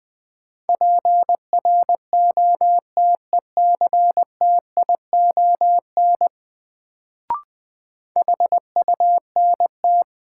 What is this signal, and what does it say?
Morse code, heard as a keyed tone, about 21 wpm, 700 Hz. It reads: PROTECTION HUNT
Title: Sentences from Top 1600 Words - New Word Drill Alt - 20wpm